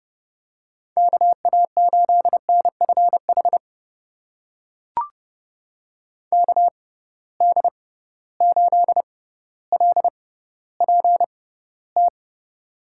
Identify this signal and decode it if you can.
Morse code, heard as a keyed tone, about 31 wpm, spaced as for 16 wpm, 700 Hz. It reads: KA8NFH KD8LPT
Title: USA Call Signs - 6 Characters - 6x Character Spacing - 30wpm